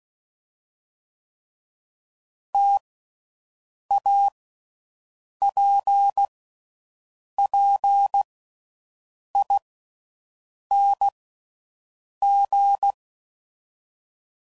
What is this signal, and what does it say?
Morse code, heard as a keyed tone, about 16 wpm, spaced as for 6 wpm, 800 Hz. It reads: TAPPING
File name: Combinations / Difficult Combos / T AOEI's